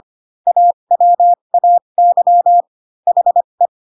Morse code, heard as a keyed tone, about 25 wpm, spaced as for 22 wpm, 700 Hz. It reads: AWAY HE
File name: Pinocchio / Pinocchio ch3 18 25 ik1hge